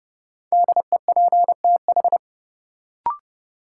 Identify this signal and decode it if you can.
Morse code, heard as a keyed tone, about 30 wpm, 700 Hz. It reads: DEPTH
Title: Sentences from Top 1200 Words - New Word Drill Alt - 30wpm